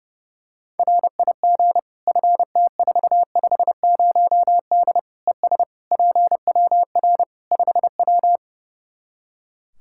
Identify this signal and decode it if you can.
Morse code, heard as a keyed tone, about 30 wpm, 700 Hz. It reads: RIG FT450D ES PWR 5W
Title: Practice QSO - Exchange 2 - Info - 30wpm